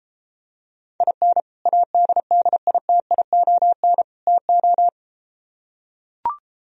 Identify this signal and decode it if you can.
Morse code, heard as a keyed tone, about 32 wpm, 700 Hz. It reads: IN ADDITION TO